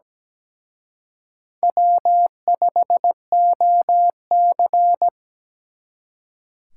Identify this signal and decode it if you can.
Morse code, heard as a keyed tone, about 20 wpm, 700 Hz. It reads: W5OC
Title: Speed Racing - USA 1x2 Call Signs 17wpm